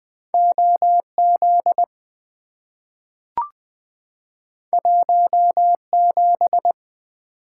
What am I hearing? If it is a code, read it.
Morse code, heard as a keyed tone, about 20 wpm, 700 Hz. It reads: OZ 17
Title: Two Letter-Number Combinations - 20wpm